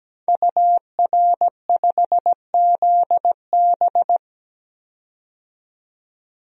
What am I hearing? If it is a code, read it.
Morse code, heard as a keyed tone, about 17 wpm, 700 Hz. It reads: UR5ZB
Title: DXCC-100 Call Signs - 5 Characters - 17wpm